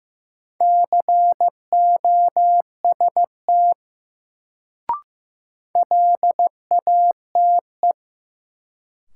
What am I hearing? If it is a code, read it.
Morse code, heard as a keyed tone, about 15 wpm, 700 Hz. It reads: COST LATE
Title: Sets of 3 Words - New Word Drill - 15wpm